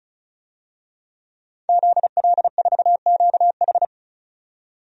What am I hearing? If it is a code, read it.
Morse code, heard as a keyed tone, about 34 wpm, 700 Hz. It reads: ZL4QH
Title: DXCC-100 Call Signs - 5 Characters - 35wpm